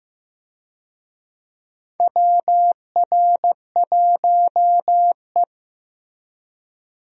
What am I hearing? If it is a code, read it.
Morse code, heard as a keyed tone, about 15 wpm, 700 Hz. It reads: WR1E
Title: USA Call Signs - 4 Characters - Repeated 3x - 15wpm